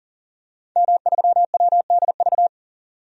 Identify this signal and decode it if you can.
Morse code, heard as a keyed tone, about 39 wpm, 700 Hz. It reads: M3WDV
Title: DXCC-100 Call Signs - 5 Characters - 40wpm